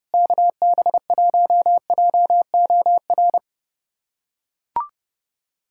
Morse code, heard as a keyed tone, about 29 wpm, 700 Hz. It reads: KB1JOR
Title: USA Callsign DE Callsign - 6 Characters - 30wpm